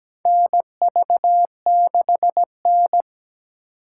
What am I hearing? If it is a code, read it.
Morse code, heard as a keyed tone, about 17 wpm, 700 Hz. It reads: NV6N
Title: USA Call Signs - 3x Character Spacing - 17wpm